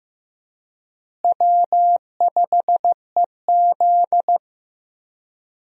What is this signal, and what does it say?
Morse code, heard as a keyed tone, about 14 wpm, 700 Hz. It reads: W5EZ
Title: USA Call Signs - 4 Characters - Repeated 3x - 15wpm